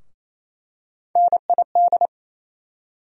Morse code, heard as a keyed tone, about 28 wpm, 700 Hz. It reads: NID
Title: USA Call Sign Postfixes - 3 Letters 28wpm